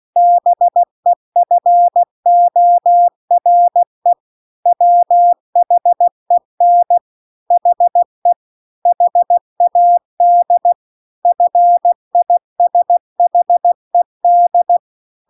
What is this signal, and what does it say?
Morse code, heard as a keyed tone, about 16 wpm, 700 Hz. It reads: BEFORE WHEN HE HAD FISHED